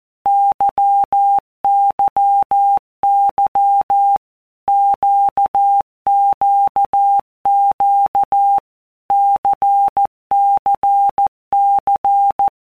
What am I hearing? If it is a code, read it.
Morse code, heard as a keyed tone, about 14 wpm, 800 Hz. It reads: YYY QQQ CCC
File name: Basic Letter Groups By WPM's / Groups@13 / LFCYQ@13